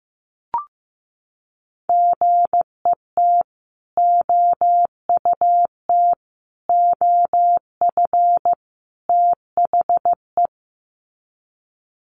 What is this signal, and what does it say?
Morse code, heard as a keyed tone, about 15 wpm, 700 Hz. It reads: GET OUT OF THE
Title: Sets of 4 Words 15wpm